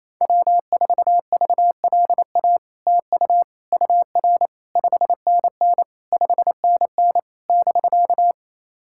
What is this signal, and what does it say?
Morse code, heard as a keyed tone, about 27 wpm, 700 Hz. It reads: W4VLA TU UR 5NN 5NN <BK>
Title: SOTA QSO Practice 28wpm